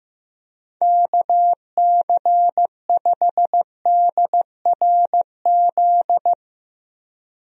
Morse code, heard as a keyed tone, about 15 wpm, 700 Hz. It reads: KC5DRZ
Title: USA Call Signs - 6 Characters - Repeated 2x - 15wpm